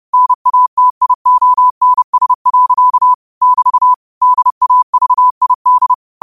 Morse code, heard as a keyed tone, about 30 wpm, 1000 Hz. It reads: NATIONS. <BT> DAVID
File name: 30wpm